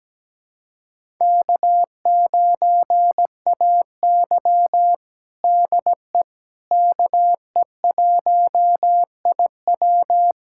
Morse code, heard as a keyed tone, about 17 wpm, 700 Hz. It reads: K9AY DE KE1IW
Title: USA Callsign DE Callsign 17wpm